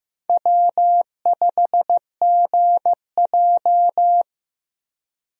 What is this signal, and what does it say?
Morse code, heard as a keyed tone, about 15 wpm, 700 Hz. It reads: W5GJ
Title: USA Call Signs - 4 Characters - Repeated 3x - 15wpm